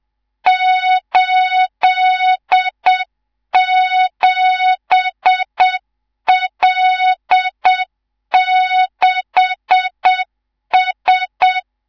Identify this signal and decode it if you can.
Morse code, heard as a keyed tone, about 7 wpm, 3700 Hz, with noise. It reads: 87L6S
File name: Morze-HA2MN / hang / 19 GYAKORLAT